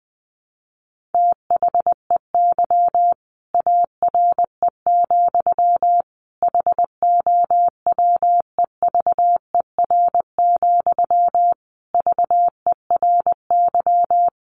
Morse code, heard as a keyed tone, about 20 wpm, 700 Hz. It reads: THEY ARE, HOWEVER, VERY
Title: Sentences from Top 1200 Words 20wpm